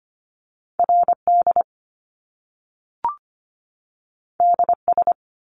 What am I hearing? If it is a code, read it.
Morse code, heard as a keyed tone, about 25 wpm, 700 Hz. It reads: R D D S
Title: Top Two Letter Combinations in Words 25wpm